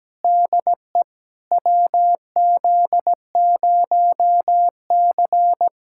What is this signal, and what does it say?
Morse code, heard as a keyed tone, about 17 wpm, 700 Hz. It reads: DE WZ0C
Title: USA Callsign DE Callsign - 4 Characters - 17wpm